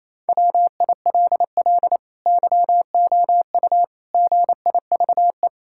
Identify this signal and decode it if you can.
Morse code, heard as a keyed tone, about 28 wpm, 700 Hz. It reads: WILL YOU GIVE
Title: Sentences from Top 1000 Words - Review Part 5 - 28wpm